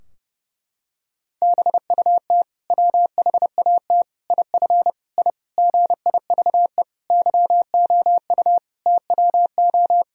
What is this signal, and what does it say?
Morse code, heard as a keyed tone, about 32 wpm, 700 Hz. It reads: BUT WHAT IF I GIVE YOU TWO
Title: Sentences from Top 1000 Words - Review Part 2 - 30wpm